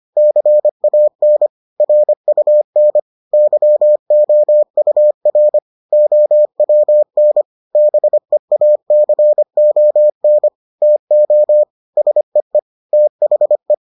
Morse code, heard as a keyed tone, about 25 wpm, 600 Hz. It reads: CAN RUN YOUR OWN BEACON TO SEE THE